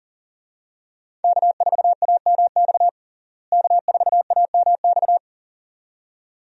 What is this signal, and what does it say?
Morse code, heard as a keyed tone, about 40 wpm, 700 Hz. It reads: K4AMX K4AMX
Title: USA Call Signs - 5 Characters - Repeated 2x - 40wpm